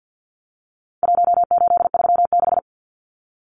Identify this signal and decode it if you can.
Morse code, heard as a keyed tone, about 50 wpm, 700 Hz. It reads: 1836
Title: Numbers - Contest Style - 4-Digit 50wpm